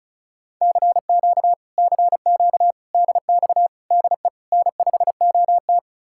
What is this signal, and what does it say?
Morse code, heard as a keyed tone, about 48 wpm, spaced as for 42 wpm, 700 Hz. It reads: CQ CQ DX DE N5OT
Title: Speed-Racing Common QSO Phrases - 35wpm